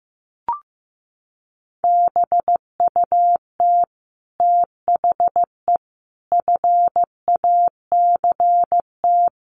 Morse code, heard as a keyed tone, about 15 wpm, 700 Hz. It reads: BUT THE FACT
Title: Sets of 5 Words Encore2 15wpm